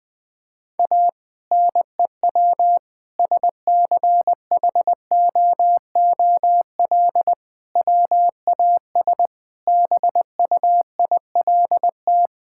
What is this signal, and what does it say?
Morse code, heard as a keyed tone, about 20 wpm, 700 Hz. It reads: A NEW SCHOOL WAS BUILT
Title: Sentences with 2-Word Phrases - Spoken Beginning - 20wpm